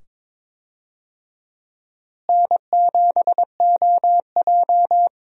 Morse code, heard as a keyed tone, about 22 wpm, 700 Hz. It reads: N7OJ
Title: USA Call Signs - 4 Characters - Repeated 2x - 22wpm